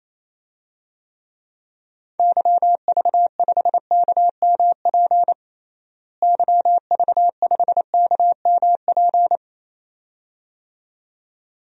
Morse code, heard as a keyed tone, about 27 wpm, 700 Hz. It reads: YV5KMP YV5KMP
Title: DXCC-100 Call Signs - 6 Characters - Repeated 2x - 28wpm